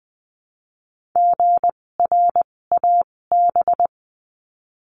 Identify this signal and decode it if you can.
Morse code, heard as a keyed tone, about 20 wpm, spaced as for 16 wpm, 700 Hz. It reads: GRAB
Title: 256 Lesson 42a - Words 12-20wpm